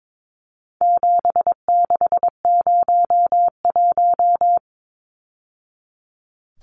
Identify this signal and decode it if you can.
Morse code, heard as a keyed tone, about 22 wpm, 700 Hz. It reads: 7601
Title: Numbers - 4-Digit 22wpm